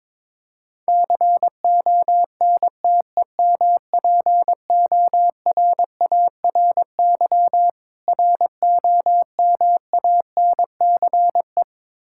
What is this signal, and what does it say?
Morse code, heard as a keyed tone, about 22 wpm, 690 Hz. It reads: CONTEMPORARY ROMANCE
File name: Difficult Words with Hints - 22wpm